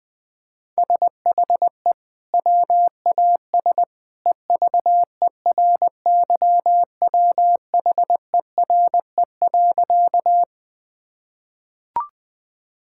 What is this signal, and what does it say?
Morse code, heard as a keyed tone, about 20 wpm, 700 Hz. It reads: SHE WAS EVERYWHERE.
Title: Sentences from Top 1600 Words 20wpm